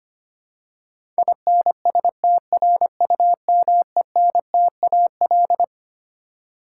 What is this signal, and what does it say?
Morse code, heard as a keyed tone, about 25 wpm, 700 Hz. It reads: INSTRUMENTAL
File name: Twelve Letter Words - 25wpm